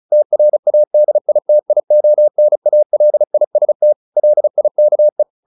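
Morse code, heard as a keyed tone, about 34 wpm, 600 Hz. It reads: TRADITIONALIST LIKE
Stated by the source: Morse practice at 35 wpm from CW med Gnister, sending Ken Follett's novel Never in English.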